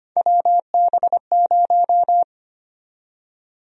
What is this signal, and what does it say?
Morse code, heard as a keyed tone, about 25 wpm, 700 Hz. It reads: WB0
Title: USA Call Sign Prefixes - 25wpm